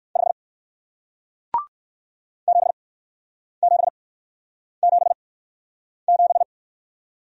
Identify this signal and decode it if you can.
Morse code, heard as a keyed tone, about 59 wpm, spaced as for 9 wpm, 700 Hz. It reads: 5 7777